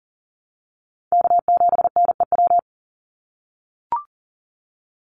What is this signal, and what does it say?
Morse code, heard as a keyed tone, about 40 wpm, 700 Hz. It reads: K7NEW
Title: NA SOTA Chaser Callsigns 40wpm